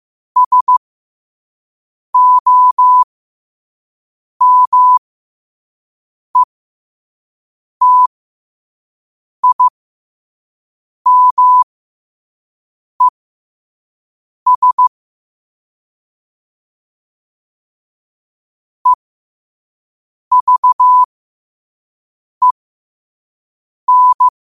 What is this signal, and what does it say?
Morse code, heard as a keyed tone, about 15 wpm, spaced as for 5 wpm, 1000 Hz. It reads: SOMETIMES EVEN
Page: New quotes every day in morse code at 5 Words per minute.